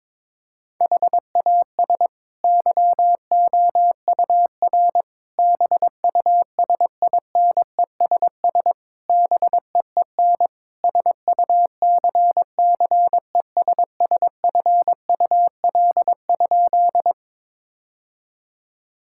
Morse code, heard as a keyed tone, about 22 wpm, 700 Hz. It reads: HAS YOUR BUSINESS BEEN SUCCESSFUL?
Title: Sentences from Top 1000 Words - Review Part 6 - 22wpm